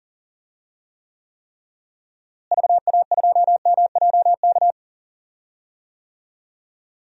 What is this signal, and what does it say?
Morse code, heard as a keyed tone, about 40 wpm, 700 Hz. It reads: VA2MJK
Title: DXCC-100 Call Signs - 6 Characters - Repeated 3x - 40wpm